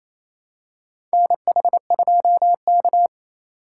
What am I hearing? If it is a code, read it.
Morse code, heard as a keyed tone, about 28 wpm, 700 Hz. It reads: NH2K
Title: USA Call Signs - 4 Characters - 6x Character Spacing - 28wpm